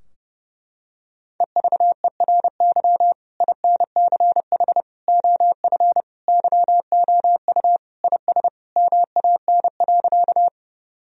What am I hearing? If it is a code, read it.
Morse code, heard as a keyed tone, about 32 wpm, 700 Hz. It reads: EVERY INCH OF YOU IS MAN.
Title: Sentences from Top 1000 Words - Review Part 3 - 30wpm